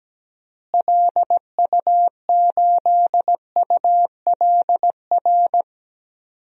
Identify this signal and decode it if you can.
Morse code, heard as a keyed tone, about 17 wpm, 700 Hz. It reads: LU8ULR
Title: DXCC-100 Call Signs - 6 Characters - Repeated 3x - 17wpm